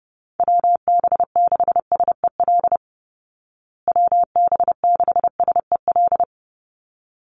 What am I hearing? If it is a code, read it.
Morse code, heard as a keyed tone, about 30 wpm, 700 Hz. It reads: WB6SEL WB6SEL
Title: Call Signs Repeated 3x - 30wpm